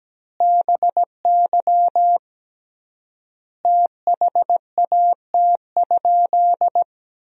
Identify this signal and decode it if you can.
Morse code, heard as a keyed tone, about 17 wpm, 700 Hz. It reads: BY THAT?
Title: 3x Word-Spacing - Sentences from Top 200 Words - 17wpm